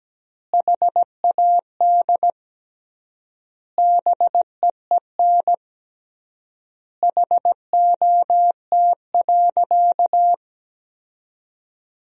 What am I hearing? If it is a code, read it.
Morse code, heard as a keyed tone, about 17 wpm, 700 Hz. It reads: HAD BEEN HOT.
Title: 3x Word-Spacing - Sentences from Top 500 Words - 17wpm